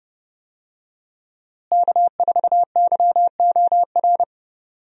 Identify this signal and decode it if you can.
Morse code, heard as a keyed tone, about 30 wpm, 700 Hz. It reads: K4YOR
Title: USA Call Signs - 5 Characters - Repeated 2x - 30wpm